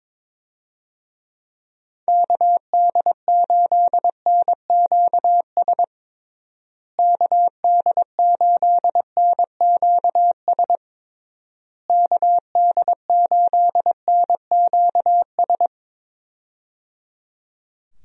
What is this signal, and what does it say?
Morse code, heard as a keyed tone, about 22 wpm, 700 Hz. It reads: KD8NQS KD8NQS KD8NQS
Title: USA Call Signs - 6 Characters - Repeated 3x - 22wpm